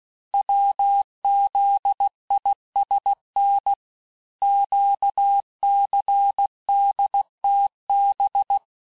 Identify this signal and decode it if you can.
Morse code, heard as a keyed tone, about 16 wpm, 800 Hz. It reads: WZISN QCDTB
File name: cours-du-lundi files / Cours-1586-texte-chiffre-vitesse-15mpm